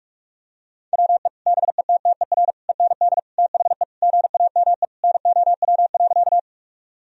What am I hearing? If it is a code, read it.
Morse code, heard as a keyed tone, about 46 wpm, 700 Hz. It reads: WE BETTER END THE GAME NOW.
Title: Sentences from Top 1000 Words - Review Part 3 - 45wpm